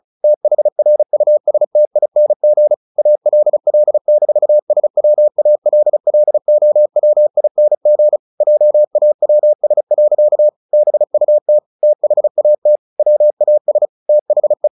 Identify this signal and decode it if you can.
Morse code, heard as a keyed tone, about 34 wpm, 600 Hz. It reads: THRUSTING ALL-SWALLOWING JAWS. BUT THAT WAS THE
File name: the old man and the sea 2 35wpm